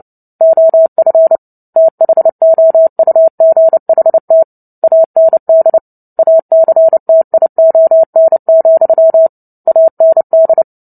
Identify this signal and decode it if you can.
Morse code, heard as a keyed tone, about 30 wpm, 650 Hz. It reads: OF THOUGHT AND ACTION, AND
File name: Book Two / 03-Chapter 3 - 29 WPM